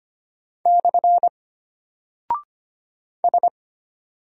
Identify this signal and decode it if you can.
Morse code, heard as a keyed tone, about 25 wpm, 700 Hz. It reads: / S
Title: Single Character (Letter:Number:Punctuation:Prosign) Alt 25wpm